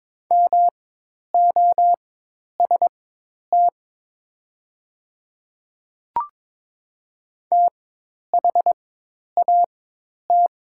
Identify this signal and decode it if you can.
Morse code, heard as a keyed tone, about 22 wpm, spaced as for 10 wpm, 700 Hz. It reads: MOST THAT
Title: 4x Character Spacing - Top 100 Words - 22wpm